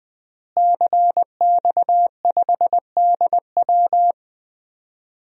Morse code, heard as a keyed tone, about 20 wpm, 700 Hz. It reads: CX5DW
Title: DXCC-100 Call Signs - 5 Characters - Repeated 3x - 20wpm